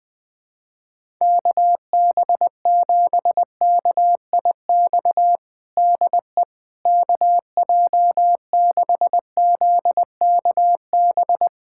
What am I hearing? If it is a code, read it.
Morse code, heard as a keyed tone, about 21 wpm, 700 Hz. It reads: KB7KIX DE KJ6ZKB
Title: USA Callsign DE Callsign 20wpm